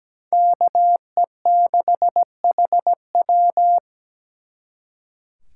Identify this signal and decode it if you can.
Morse code, heard as a keyed tone, about 20 wpm, 700 Hz. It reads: KE6HW
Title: Speed Racing - USA 2x2 Call Signs 17wpm